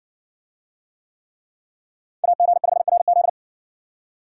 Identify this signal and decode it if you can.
Morse code, heard as a keyed tone, about 57 wpm, 700 Hz. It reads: AG5RZ